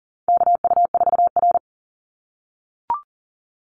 Morse code, heard as a keyed tone, about 40 wpm, 700 Hz. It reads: KU4R
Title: NA SOTA Chaser Callsigns 40wpm